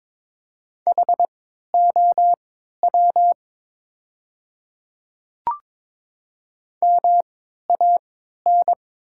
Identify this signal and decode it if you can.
Morse code, heard as a keyed tone, about 23 wpm, spaced as for 13 wpm, 700 Hz. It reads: HOW MAN
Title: 3x Character Spacing - Top 300 Words - 22wpm